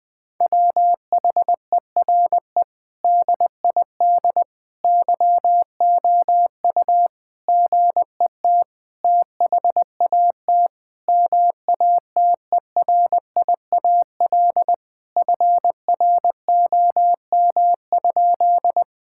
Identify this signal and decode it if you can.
Morse code, heard as a keyed tone, about 20 wpm, 700 Hz. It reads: WHERE DID YOU GET THAT MATERIAL FROM?
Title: Sentences from Top 1000 Words - Review Part 5 - 20wpm